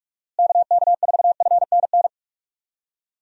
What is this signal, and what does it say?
Morse code, heard as a keyed tone, about 46 wpm, 700 Hz. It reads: KK4FNN
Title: USA Call Signs - 6 Characters - Repeated 2x - 45wpm